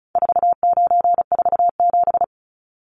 Morse code, heard as a keyed tone, about 34 wpm, 700 Hz. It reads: 4947
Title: Numbers - 4-Digit 35wpm